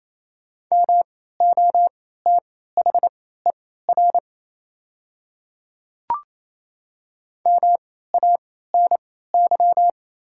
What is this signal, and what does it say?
Morse code, heard as a keyed tone, about 28 wpm, spaced as for 16 wpm, 700 Hz. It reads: MOTHER MANY
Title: 3x Character Spacing - Top 200 Words - 28wpm